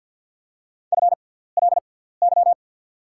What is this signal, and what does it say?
Morse code, heard as a keyed tone, about 49 wpm, spaced as for 17 wpm, 690 Hz. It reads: FLY
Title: 6x Character Spacing - Top 500 Words - 50wpm